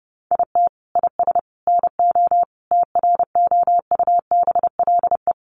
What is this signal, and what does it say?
Morse code, heard as a keyed tone, about 30 wpm, 700 Hz. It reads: IT IS NO TROUBLE
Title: Sentences from Top 1200 Words 30wpm